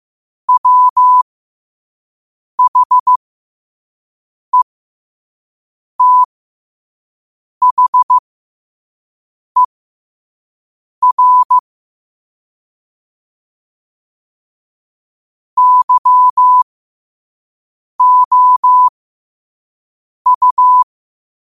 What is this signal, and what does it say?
Morse code, heard as a keyed tone, about 15 wpm, spaced as for 5 wpm, 1000 Hz. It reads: WHETHER YOU
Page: New quotes every day in morse code at 5 Words per minute.